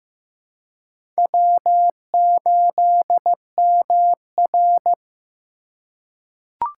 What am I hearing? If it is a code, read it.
Morse code, heard as a keyed tone, about 15 wpm, 700 Hz. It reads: W8MR
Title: USA Call Signs - 3x Character Spacing - 15wpm